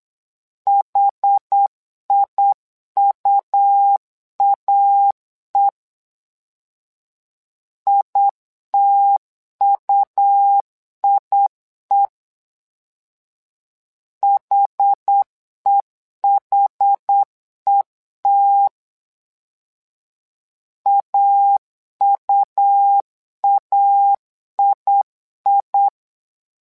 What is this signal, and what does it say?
Morse code, heard as a keyed tone, about 8 wpm, 800 Hz. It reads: HIUAE ITUIE HEHET AUAII
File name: LEZIONI / 2 UHDN / 2 UH 8wpm 2129